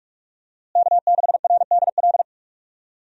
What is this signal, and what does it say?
Morse code, heard as a keyed tone, about 42 wpm, 700 Hz. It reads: K6RDL
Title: USA Call Signs - 5 Characters - Repeated 3x - 45wpm